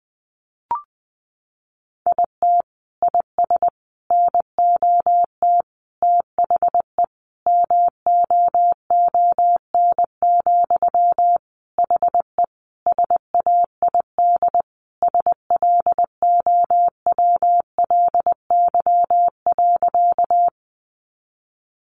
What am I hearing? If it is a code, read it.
Morse code, heard as a keyed tone, about 20 wpm, 700 Hz. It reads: IT IS NOT THE MOON, HE SAID SLOWLY.
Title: Sentences from Top 1100 Words 20wpm